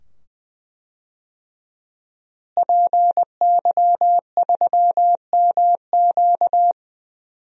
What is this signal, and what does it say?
Morse code, heard as a keyed tone, about 20 wpm, 700 Hz. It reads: PY3MQ
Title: DXCC-100 Call Signs - 5 Characters - Repeated 2x - 20wpm